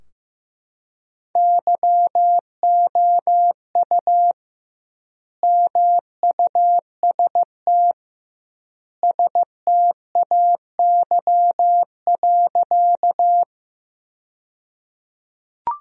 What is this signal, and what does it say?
Morse code, heard as a keyed tone, about 14 wpm, 700 Hz. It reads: YOU MUST STAY.
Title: 2x Word-Spacing - Sentences from Top 500 Words - 15wpm